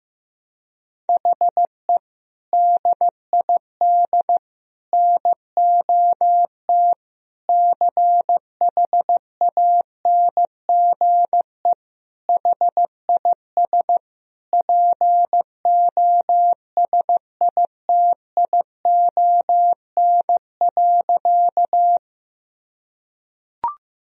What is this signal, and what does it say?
Morse code, heard as a keyed tone, about 16 wpm, 700 Hz. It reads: HE DID NOT CHANGE HIS POSITION.
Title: Sentences from Top 1000 Words - Review Part 1 - 15wpm